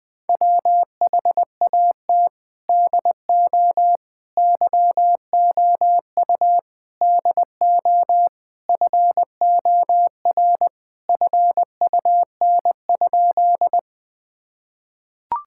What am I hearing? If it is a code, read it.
Morse code, heard as a keyed tone, about 21 wpm, 700 Hz. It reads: WHAT DO YOU DO FOR FUN?
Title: Sentences with 4-Word Phrases - 20wpm